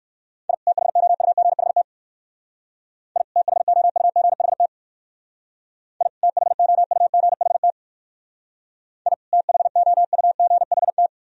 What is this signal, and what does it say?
Morse code, heard as a keyed tone, about 59 wpm, 690 Hz. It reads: I THOUGHT I THOUGHT I THOUGHT I THOUGHT